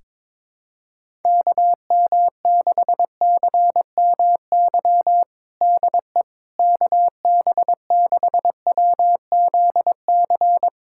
Morse code, heard as a keyed tone, about 21 wpm, 700 Hz. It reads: KM6CMY DE KB6WZC
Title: USA Callsign DE Callsign - 6 Characters - 22wpm